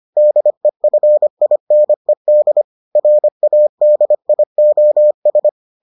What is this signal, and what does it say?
Morse code, heard as a keyed tone, about 25 wpm, 600 Hz. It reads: DEFINED RADIOS